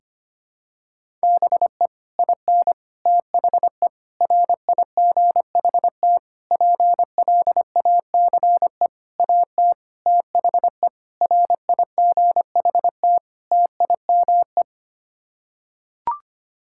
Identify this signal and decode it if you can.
Morse code, heard as a keyed tone, about 25 wpm, 700 Hz. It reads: BE IN THE RIGHT PLACE AT THE RIGHT TIME
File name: Sets of 9 Words - 25wpm